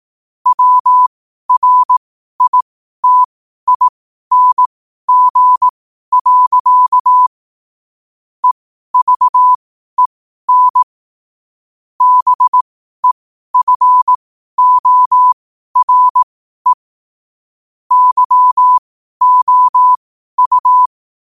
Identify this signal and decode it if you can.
Morse code, heard as a keyed tone, about 18 wpm, spaced as for 12 wpm, 1000 Hz. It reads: WRITING. EVEN BEFORE YOU
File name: qotdfiles / 12wpm